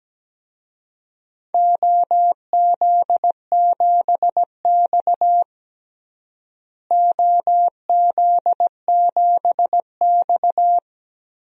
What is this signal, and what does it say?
Morse code, heard as a keyed tone, about 17 wpm, 700 Hz. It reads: OZ7X OZ7X
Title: DXCC-100 Call Signs - 4 Characters - Repeated 3x - 17wpm